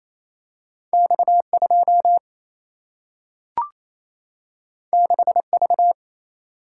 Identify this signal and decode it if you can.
Morse code, heard as a keyed tone, about 28 wpm, 700 Hz. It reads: X2 6V
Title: Single-Letter Single-Number 28wpm